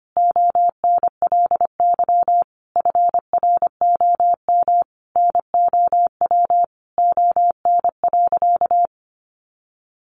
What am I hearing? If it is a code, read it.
Morse code, heard as a keyed tone, about 25 wpm, 700 Hz. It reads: ONLY FROM NOW ON.
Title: Easy Phrases 25wpm